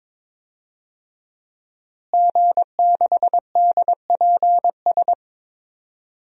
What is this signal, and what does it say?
Morse code, heard as a keyed tone, about 23 wpm, 700 Hz. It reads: G6DPS
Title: DXCC-100 Call Signs - 5 Characters - Repeated 2x - 22wpm